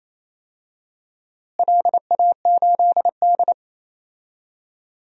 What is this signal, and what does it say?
Morse code, heard as a keyed tone, about 28 wpm, 700 Hz. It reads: LA8D
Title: DXCC-100 Call Signs - 4 Characters - Repeated 2x - 28wpm